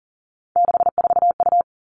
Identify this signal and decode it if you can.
Morse code, heard as a keyed tone, about 42 wpm, 700 Hz. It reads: 64U
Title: Three Challenging Letter-Number Combinations 40wpm